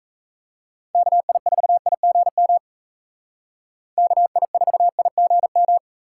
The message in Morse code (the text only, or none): KI4IGM KI4IGM